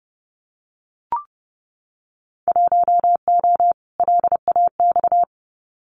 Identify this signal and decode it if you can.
Morse code, heard as a keyed tone, about 30 wpm, 700 Hz. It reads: 1O LAX
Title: Winter Field Day Exchanges 30wpm